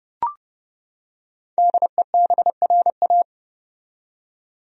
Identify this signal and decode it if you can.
Morse code, heard as a keyed tone, about 30 wpm, 700 Hz. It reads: DEBRA
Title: USA Common Names - 30wpm